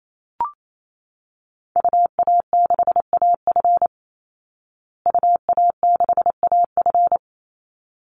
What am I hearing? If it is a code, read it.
Morse code, heard as a keyed tone, about 28 wpm, 700 Hz. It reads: UA6AF UA6AF
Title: Call Signs Repeated 3x - 28wpm